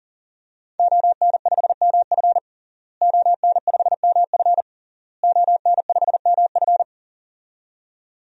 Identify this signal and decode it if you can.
Morse code, heard as a keyed tone, about 38 wpm, 700 Hz. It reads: ON5MF ON5MF ON5MF
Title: DXCC-100 Call Signs - 5 Characters - Repeated 3x - 40wpm